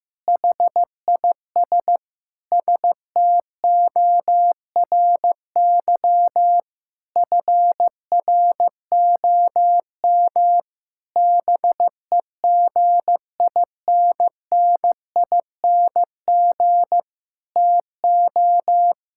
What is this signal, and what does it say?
Morse code, heard as a keyed tone, about 16 wpm, 700 Hz. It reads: HIS STORY FROM BEGINNING TO
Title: Sentences from Top 1000 Words - Review Part 5 - 15wpm